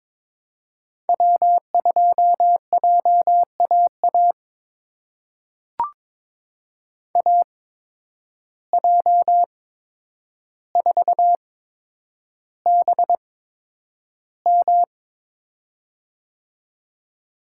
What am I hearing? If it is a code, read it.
Morse code, heard as a keyed tone, about 22 wpm, spaced as for 9 wpm, 700 Hz. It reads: W2JAA AJ4BM
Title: USA Call Signs - 5 Characters - 8x Character Spacing - 22wpm